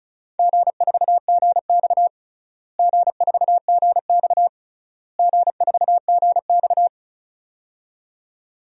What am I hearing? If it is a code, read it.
Morse code, heard as a keyed tone, about 36 wpm, 700 Hz. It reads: G4GX G4GX G4GX
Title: DXCC-100 Call Signs - 4 Characters - Repeated 3x - 35wpm